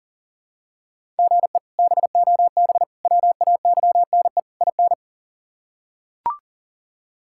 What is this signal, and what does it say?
Morse code, heard as a keyed tone, about 40 wpm, 700 Hz. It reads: GE BOB WAYNE IN
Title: K1USN SST QSO Practice - 40wpm